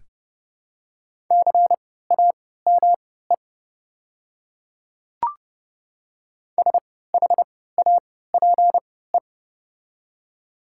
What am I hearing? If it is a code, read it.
Morse code, heard as a keyed tone, about 30 wpm, spaced as for 17 wpm, 700 Hz. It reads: CAME SHAPE
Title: 3x Character Spacing - Top 500 Words - 30wpm